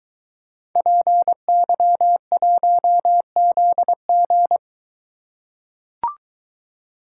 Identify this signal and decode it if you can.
Morse code, heard as a keyed tone, about 22 wpm, 700 Hz. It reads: PY1ZG